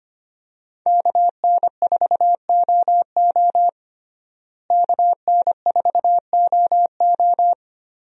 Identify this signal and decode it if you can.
Morse code, heard as a keyed tone, about 25 wpm, 700 Hz. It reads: KN4OO KN4OO
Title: USA Call Signs - 5 Characters - Repeated 3x - 25wpm